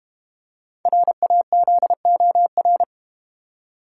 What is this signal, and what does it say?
Morse code, heard as a keyed tone, about 32 wpm, 700 Hz. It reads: RAZOR